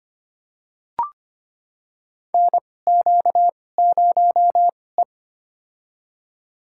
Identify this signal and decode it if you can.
Morse code, heard as a keyed tone, about 25 wpm, spaced as for 18 wpm, 700 Hz. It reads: NQ0E
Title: USA Call Signs - 4 Characters - 2x Character Spacing - 25wpm